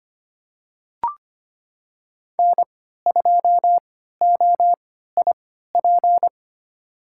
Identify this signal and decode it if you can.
Morse code, heard as a keyed tone, about 25 wpm, spaced as for 14 wpm, 700 Hz. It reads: N2OIP
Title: USA Call Signs - 3x Character Spacing - 25wpm